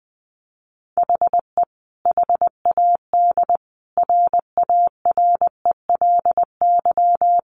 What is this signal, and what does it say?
Morse code, heard as a keyed tone, about 20 wpm, 700 Hz. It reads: HE HAD RARELY
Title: Sentences from Top 1400 Words 20wpm